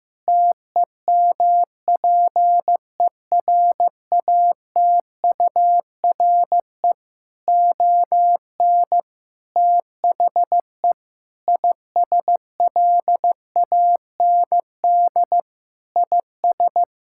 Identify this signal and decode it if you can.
Morse code, heard as a keyed tone, about 16 wpm, 700 Hz. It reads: TEMPERATURE ON THE ISLAND IS
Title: Sentences from Top 1000 Words - Review Part 6 - 15wpm